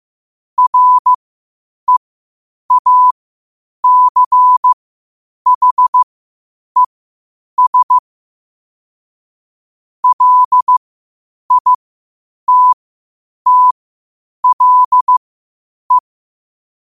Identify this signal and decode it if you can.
Morse code, heard as a keyed tone, about 15 wpm, spaced as for 8 wpm, 1000 Hz. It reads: REACHES LITTLE
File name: qotdfiles / 8wpm